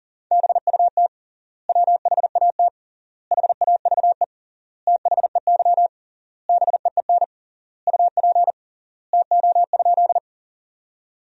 Morse code, heard as a keyed tone, about 37 wpm, 700 Hz. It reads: BUT WHAT HAVE THEY BEEN UP TO?
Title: 3x Word-Spacing - Sentences from Top 100 Words - 40wpm